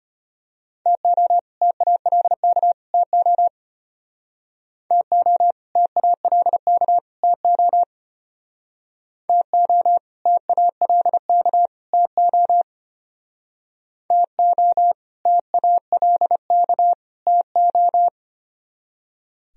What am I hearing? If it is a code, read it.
Morse code, heard as a keyed tone, about 33 wpm, 700 Hz. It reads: TO TALK TO TO TALK TO TO TALK TO TO TALK TO